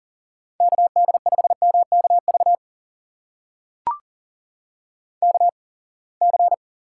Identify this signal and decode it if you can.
Morse code, heard as a keyed tone, about 40 wpm, spaced as for 28 wpm, 700 Hz. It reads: KD5MKV KC
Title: USA Call Signs - 6 Characters - 8x Character Spacing - 40wpm